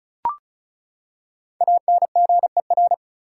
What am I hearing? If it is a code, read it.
Morse code, heard as a keyed tone, about 36 wpm, 700 Hz. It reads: ANGER
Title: Five Letter Words - 35wpm